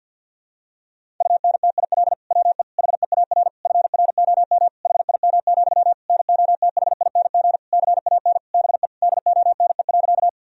Morse code, heard as a keyed tone, about 52 wpm, 700 Hz. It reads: UNTIL WE HEAR FROM HIM, NOTHING CAN BE DONE.
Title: Sentences from Top 1000 Words - Review Part 1 - 50wpm